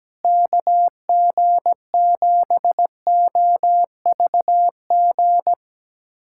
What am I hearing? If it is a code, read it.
Morse code, heard as a keyed tone, about 17 wpm, 700 Hz. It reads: KG7OVG
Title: USA Callsign DE Callsign - 6 Characters - 17wpm